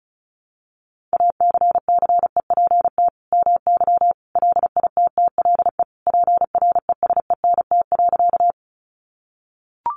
Morse code, heard as a keyed tone, about 34 wpm, 700 Hz. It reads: ACCEPT MY LITTLE PRESENT.
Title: Sentences from Top 1300 Words 35wpm